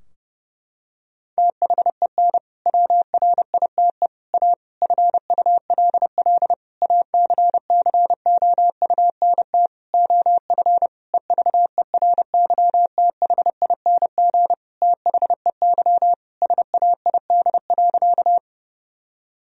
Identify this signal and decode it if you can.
Morse code, heard as a keyed tone, about 30 wpm, 700 Hz. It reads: THEN WRITE A FULL ACCOUNT OF EVERYTHING THEY SAID.
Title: Sentences from Top 1000 Words - Review Part 3 - 30wpm